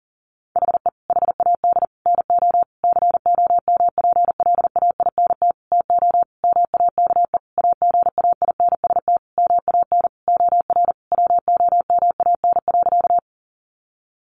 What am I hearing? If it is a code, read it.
Morse code, heard as a keyed tone, about 40 wpm, 700 Hz. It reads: HE HAD NO COMPLAINT TO MAKE AGAINST MAN OR WOMAN.
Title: Sentences from Top 1400 Words 40wpm